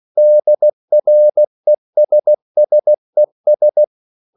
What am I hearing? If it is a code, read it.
Morse code, heard as a keyed tone, about 16 wpm, 600 Hz. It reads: DRESSES